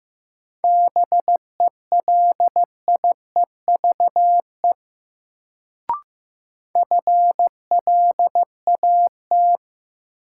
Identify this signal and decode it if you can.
Morse code, heard as a keyed tone, about 15 wpm, 700 Hz. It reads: BELIEVE FLAT
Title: Sentences from Top 1000 Words - New Word Drill - 15wpm